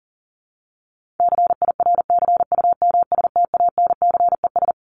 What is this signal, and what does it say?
Morse code, heard as a keyed tone, about 37 wpm, 700 Hz. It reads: CIRCUMSTANCES
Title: ES:S Suffix - Top 100 Words with Base Words 40wpm